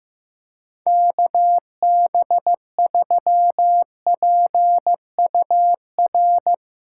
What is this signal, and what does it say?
Morse code, heard as a keyed tone, about 15 wpm, 700 Hz. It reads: KB3PUR
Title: USA Callsign DE Callsign - 6 Characters - 15wpm